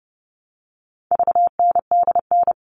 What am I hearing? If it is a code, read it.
Morse code, heard as a keyed tone, about 30 wpm, 700 Hz. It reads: VNDN
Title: All Cut Numbers - 4-Digit 30wpm